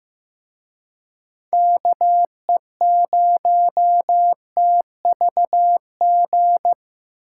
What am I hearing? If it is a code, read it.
Morse code, heard as a keyed tone, about 18 wpm, 700 Hz. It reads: KE0TVG
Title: Speed Racing - USA 2x3 Call Signs 15wpm